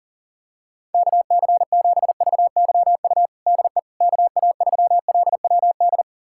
Morse code, heard as a keyed tone, about 40 wpm, 700 Hz. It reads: KC7VYU DE KA3LWD
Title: USA Callsign DE Callsign 40wpm